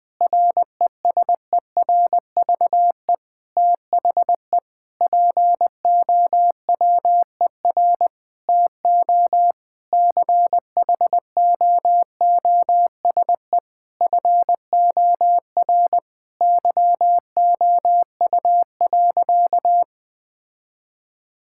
Morse code, heard as a keyed tone, about 20 wpm, 700 Hz. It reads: RESERVE THE POWER TO CHOOSE FOR YOU.
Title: Sentences from Top 1000 Words - Review Part 6 - 20wpm